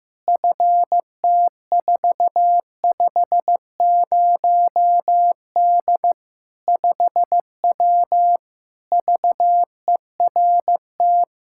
Morse code, heard as a keyed tone, about 16 wpm, 700 Hz. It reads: FT450D 5W VERT
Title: Practice QSO - Exchange 2 - Info - 15wpm